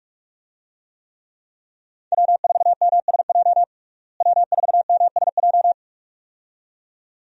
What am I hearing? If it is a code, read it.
Morse code, heard as a keyed tone, about 43 wpm, 700 Hz. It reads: W4MSJ W4MSJ
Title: USA Call Signs - 5 Characters - Repeated 2x - 45wpm